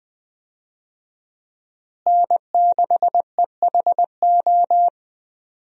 Morse code, heard as a keyed tone, about 20 wpm, 700 Hz. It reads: N6EHO
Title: USA Call Signs - 5 Characters - Repeated 2x - 20wpm